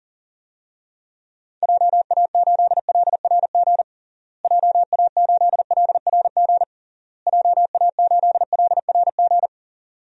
Morse code, heard as a keyed tone, about 40 wpm, 700 Hz. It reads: JA8LRG JA8LRG JA8LRG
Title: DXCC-100 Call Signs - 6 Characters - Repeated 3x - 40wpm